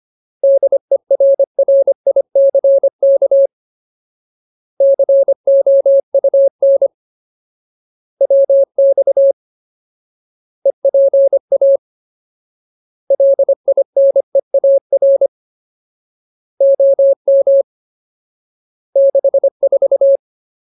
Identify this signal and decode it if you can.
Morse code, heard as a keyed tone, about 25 wpm, 550 Hz. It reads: DERRICK COUN WX EPA LINEAR OM 64
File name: Ultimate Word List / 25WPM Ultimate